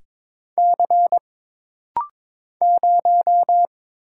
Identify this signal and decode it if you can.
Morse code, heard as a keyed tone, about 22 wpm, 700 Hz. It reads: C 0
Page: This rapid-fire timing is geared to push your ICR (Instant Character Recognition) skills.